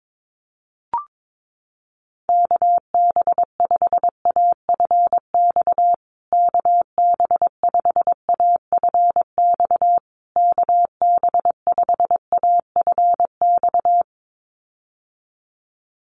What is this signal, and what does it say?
Morse code, heard as a keyed tone, about 22 wpm, 700 Hz. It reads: KB5AFX KB5AFX KB5AFX
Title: USA Call Signs Repeated 3x 22wpm